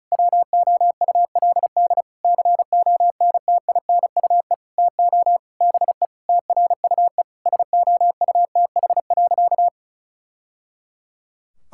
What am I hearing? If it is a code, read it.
Morse code, heard as a keyed tone, about 36 wpm, 700 Hz. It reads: WOULD CONTINUE TO BE TRUE SOUTH.
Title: Sentences from Top 1000 Words - Review Part 6 - 35wpm